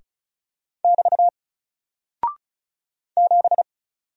Morse code, heard as a keyed tone, about 36 wpm, 700 Hz. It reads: <BT> 7
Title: Single Character (Letter:Number:Punctuation:Prosign) Alt 35wpm